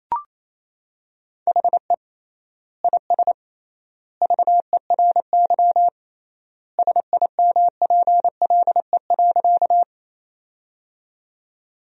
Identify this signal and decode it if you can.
Morse code, heard as a keyed tone, about 28 wpm, 700 Hz. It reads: HE IS VERY SIMPLE.
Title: 3x Word-Spacing - Sentences from Top 400 Words - 28wpm